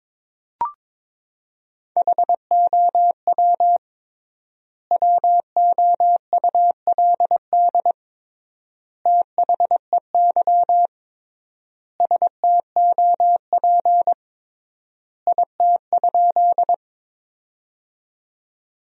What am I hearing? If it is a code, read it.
Morse code, heard as a keyed tone, about 22 wpm, 700 Hz. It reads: HOW WOULD THEY STOP IT?
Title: 3x Word-Spacing - Sentences from Top 300 Words - 22wpm